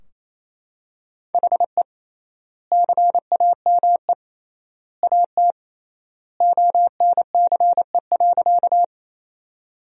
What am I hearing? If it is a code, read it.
Morse code, heard as a keyed tone, about 28 wpm, 700 Hz. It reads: HE CAME AT ONCE.
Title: 3x Word-Spacing - Sentences from Top 300 Words - 28wpm